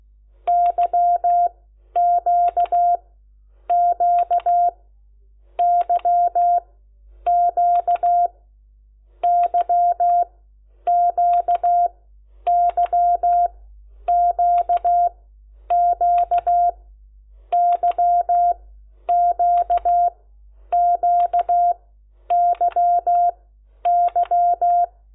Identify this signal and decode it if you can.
Morse code, heard as a keyed tone, about 16 wpm, spaced as for 9 wpm, 700 Hz. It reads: YQQYQ YQYQQ YQQYY
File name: Lecciones